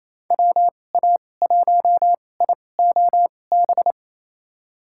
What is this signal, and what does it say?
Morse code, heard as a keyed tone, about 28 wpm, spaced as for 20 wpm, 700 Hz. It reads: WA1IOB
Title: USA Call Signs - 6 Characters - 2x Character Spacing - 28wpm